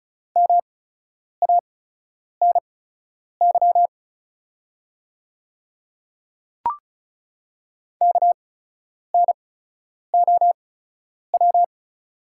Morse code, heard as a keyed tone, about 36 wpm, spaced as for 10 wpm, 690 Hz. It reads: MANY KNOW
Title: 8x Character Spacing - Top 100 Words - 35wpm